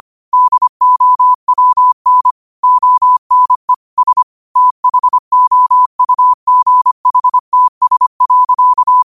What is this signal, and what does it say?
Morse code, heard as a keyed tone, about 25 wpm, 1000 Hz. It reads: DOWN ONE S THOUGHTS.